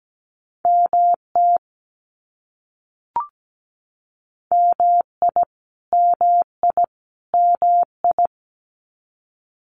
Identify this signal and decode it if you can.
Morse code, heard as a keyed tone, about 17 wpm, 700 Hz. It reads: MT MI MI MI
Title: US States Repeated 3x 17wpm